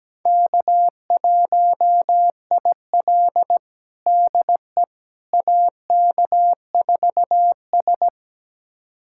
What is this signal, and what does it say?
Morse code, heard as a keyed tone, about 17 wpm, 700 Hz. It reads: K1IL DE AK4S
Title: USA Callsign DE Callsign - 4 Characters - 17wpm